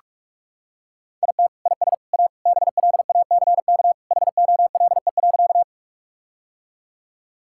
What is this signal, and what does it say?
Morse code, heard as a keyed tone, about 48 wpm, 700 Hz. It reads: IT IS A BLACK HOLE.
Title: Sentences from Top 1600 Words 45wpm